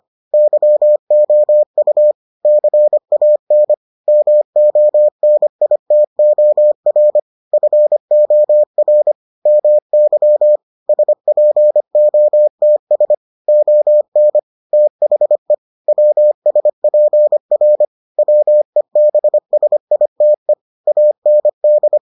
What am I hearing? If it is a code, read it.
Morse code, heard as a keyed tone, about 25 wpm, 600 Hz. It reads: YOU CAN MONITOR FOR MY SPOTS ON THE WSPR WEBSITE AND